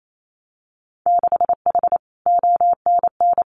Morse code, heard as a keyed tone, about 28 wpm, 700 Hz. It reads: 6H ONN
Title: Winter Field Day Exchanges Encore 28wpm